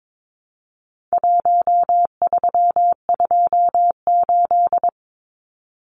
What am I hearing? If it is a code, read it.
Morse code, heard as a keyed tone, about 22 wpm, 700 Hz. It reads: 1328
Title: Numbers - Contest Style - 4-Digit 22wpm